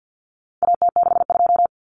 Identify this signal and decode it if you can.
Morse code, heard as a keyed tone, about 56 wpm, 700 Hz. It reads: AT62
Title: Common Cut Numbers - 4-Digit 50wpm